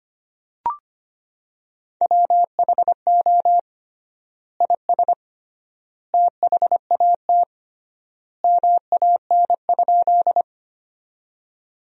Morse code, heard as a keyed tone, about 25 wpm, 700 Hz. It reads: WHO IS THAT MAN?
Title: 3x Word-Spacing - Sentences from Top 200 Words - 25wpm